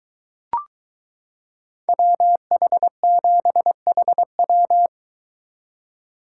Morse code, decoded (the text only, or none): WH7HW